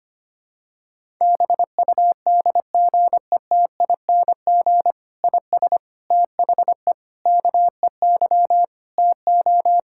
Morse code, heard as a keyed tone, about 25 wpm, 700 Hz. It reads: BUDGETING IS THE KEY TO
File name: practice / 25wpm / Difficult Sentences with Hints - 25wpm